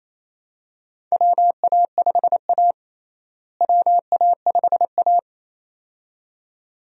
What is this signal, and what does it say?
Morse code, heard as a keyed tone, about 28 wpm, 700 Hz. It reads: WA5A WA5A
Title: USA Call Signs - 4 Characters - Repeated 2x - 28wpm